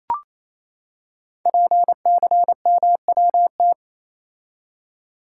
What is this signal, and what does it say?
Morse code, heard as a keyed tone, about 27 wpm, 700 Hz. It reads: PCMWT
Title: Code Groups - 5 Letters - 28wpm